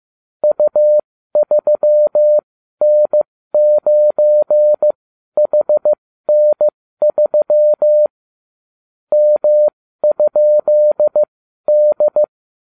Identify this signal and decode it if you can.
Morse code, heard as a keyed tone, about 15 wpm, spaced as for 12 wpm, 600 Hz. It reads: U3N9HN3 M?D